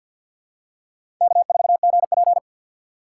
Morse code, heard as a keyed tone, about 52 wpm, 700 Hz. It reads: K4GP
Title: USA Call Signs - 4 Characters - 6x Character Spacing - 50wpm